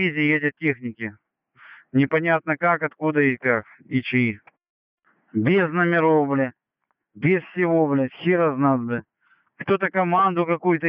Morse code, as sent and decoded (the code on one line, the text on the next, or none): none